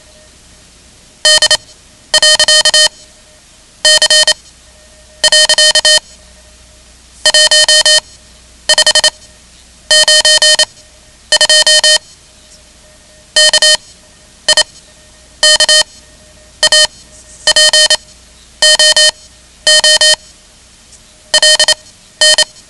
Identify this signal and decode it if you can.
Morse code, heard as a keyed tone, about 27 wpm, spaced as for 10 wpm, 3050 Hz. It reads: D.C. 1592 KIKAPOO LN
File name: ham radio / CodeQuick / Code Quick 10